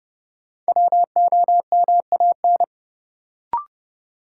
Morse code, heard as a keyed tone, about 30 wpm, 700 Hz. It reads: WOMAN
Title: Sentences from Top 1000 Words - Review Part 2 - New Word Drill - 30wpm